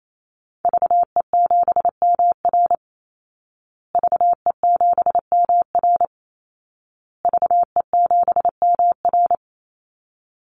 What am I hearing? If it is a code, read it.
Morse code, heard as a keyed tone, about 28 wpm, 700 Hz. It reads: VE7MR VE7MR VE7MR
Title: Call Signs Repeated 2x - 28wpm